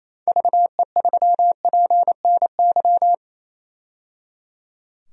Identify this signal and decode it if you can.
Morse code, heard as a keyed tone, about 28 wpm, 700 Hz. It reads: VE3PNY
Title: DXCC-100 Call Signs - 6 Characters - 28wpm